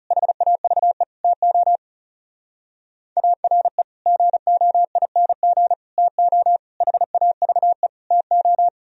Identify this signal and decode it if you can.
Morse code, heard as a keyed tone, about 37 wpm, 700 Hz. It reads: HAVE TO ARE GOING TO HAVE TO
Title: Speed-Racing - Sets of 5 Words 30wpm